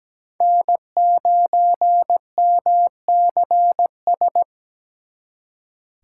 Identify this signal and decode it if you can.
Morse code, heard as a keyed tone, about 20 wpm, 700 Hz. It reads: N9MCS
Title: Speed Racing - USA 1x3 Call Signs 17wpm